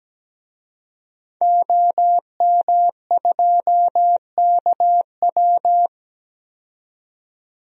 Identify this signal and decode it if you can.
Morse code, heard as a keyed tone, about 17 wpm, 700 Hz. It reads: OM2KW
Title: DXCC-100 Call Signs - 5 Characters - 17wpm